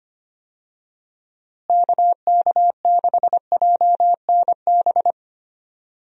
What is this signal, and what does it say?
Morse code, heard as a keyed tone, about 25 wpm, 700 Hz. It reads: KK6JNB
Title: USA Call Signs - 6 Characters - Repeated 3x - 25wpm